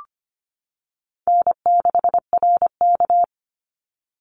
Morse code, heard as a keyed tone, about 25 wpm, 700 Hz. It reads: N6RK
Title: Call Signs - 25wpm